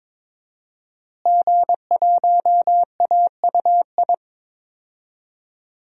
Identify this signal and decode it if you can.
Morse code, heard as a keyed tone, about 22 wpm, 700 Hz. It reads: G1AUI
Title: DXCC-100 Call Signs - 5 Characters - Repeated 2x - 22wpm